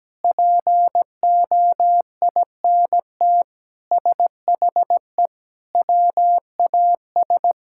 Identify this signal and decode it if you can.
Morse code, heard as a keyed tone, about 17 wpm, 700 Hz. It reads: POINT SHE WAS
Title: Sentences from Top 1000 Words - Review Part 6 - 17wpm